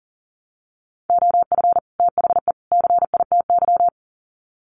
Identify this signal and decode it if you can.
Morse code, heard as a keyed tone, about 40 wpm, 700 Hz. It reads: OF THE CITY
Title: Sets of 3 Words 40wpm